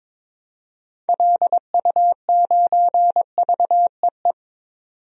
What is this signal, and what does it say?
Morse code, heard as a keyed tone, about 22 wpm, 700 Hz. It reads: LU9VEE
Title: DXCC-100 Call Signs - 6 Characters - 22wpm